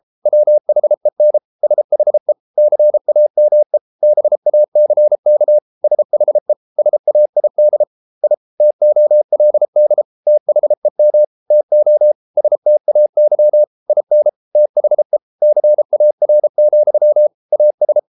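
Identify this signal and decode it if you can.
Morse code, heard as a keyed tone, about 33 wpm, 600 Hz. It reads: WHEN SHE CAME BACK SHE SAID I TOLD THEM TO STAY IN THE CAR, AS
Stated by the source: Morse code practice at 32 wpm from CW med Gnister, reading Ken Follett's novel Never in English.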